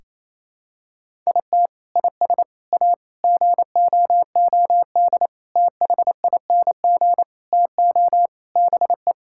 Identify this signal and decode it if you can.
Morse code, heard as a keyed tone, about 28 wpm, 700 Hz. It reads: IT IS A GOOD THING TO BE
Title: Sentences from Top 1000 Words - Review Part 5 - 28wpm